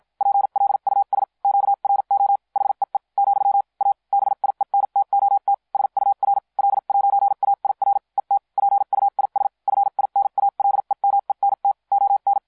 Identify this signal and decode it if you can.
Morse code, heard as a keyed tone, about 56 wpm, spaced as for 45 wpm, 800 Hz. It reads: PLUS QRO HEE , A BIENTOT SUR L'AIR ET PUIS FINALEMENT ON